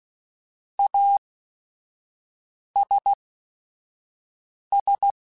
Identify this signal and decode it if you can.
Morse code, heard as a keyed tone, about 16 wpm, spaced as for 5 wpm, 800 Hz. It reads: ASS
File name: QSOs / QSOs-mp3s / kc7sac@5